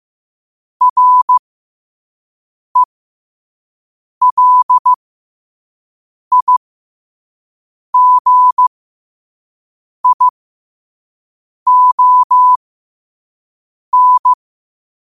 Morse code, heard as a keyed tone, about 15 wpm, spaced as for 5 wpm, 1000 Hz. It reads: RELIGION
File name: qotdfiles / 5wpm